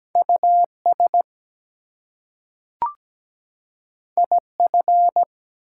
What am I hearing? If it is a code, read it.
Morse code, heard as a keyed tone, about 17 wpm, 700 Hz. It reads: US IF
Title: Two Letter Words - 17wpm